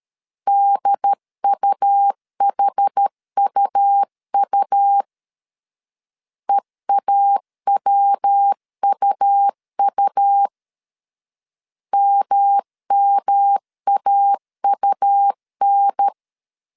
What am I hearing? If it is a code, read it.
Morse code, heard as a keyed tone, about 13 wpm, 800 Hz. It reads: DUHUU EAWUU MMAUN
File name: CW / DET / 1 3 1500C S A 12wpm 2905